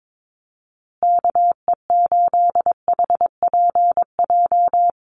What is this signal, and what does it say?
Morse code, heard as a keyed tone, about 22 wpm, 700 Hz. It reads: KE8HPJ
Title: USA Call Signs Repeated 3x 22wpm